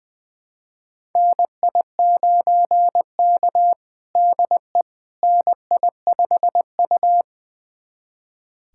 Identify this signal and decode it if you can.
Morse code, heard as a keyed tone, about 21 wpm, 700 Hz. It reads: NI9K DE NI5U
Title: USA Callsign DE Callsign - 4 Characters - 20wpm